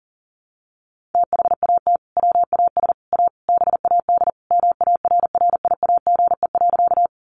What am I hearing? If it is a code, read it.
Morse code, heard as a keyed tone, about 37 wpm, 700 Hz. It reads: THAT WAS A BAD MARRIAGE.
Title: Sentences from Top 800 Words 40wpm